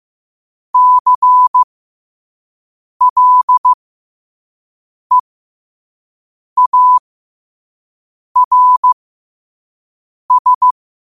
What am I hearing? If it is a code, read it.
Morse code, heard as a keyed tone, about 15 wpm, spaced as for 5 wpm, 1000 Hz. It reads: CLEARS
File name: qotdfiles / 5wpm